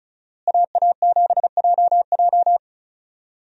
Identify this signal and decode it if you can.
Morse code, heard as a keyed tone, about 34 wpm, 700 Hz. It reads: AA7JJ
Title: USA Call Signs - 5 Characters - Repeated 3x - 35wpm